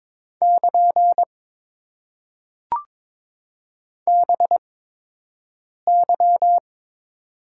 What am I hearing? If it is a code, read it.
Morse code, heard as a keyed tone, about 22 wpm, spaced as for 6 wpm, 700 Hz. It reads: <KN> BY
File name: Top 90 QSO Elements - 8x Character Spacing - 22wpm